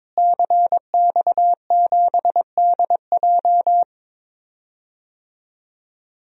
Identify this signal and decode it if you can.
Morse code, heard as a keyed tone, about 21 wpm, 700 Hz. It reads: CX7DJ
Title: DXCC-100 Call Signs - 5 Characters - Repeated 3x - 22wpm